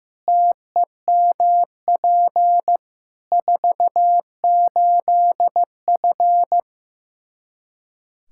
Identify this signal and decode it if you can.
Morse code, heard as a keyed tone, about 16 wpm, 700 Hz. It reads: TEMP 48F
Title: Practice QSO - Exchange 2 - Info - 15wpm